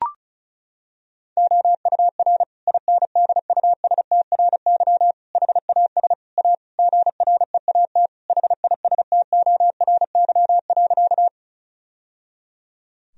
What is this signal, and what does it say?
Morse code, heard as a keyed tone, about 36 wpm, 700 Hz. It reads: OUR INDUSTRY HAS A GREAT HISTORY.
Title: Sentences from Top 1000 Words - Review Part 4 - 35wpm